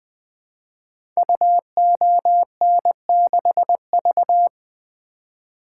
Morse code, heard as a keyed tone, about 20 wpm, 700 Hz. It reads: UON6V
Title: Code Groups - 5 Letter:Numbers - 20wpm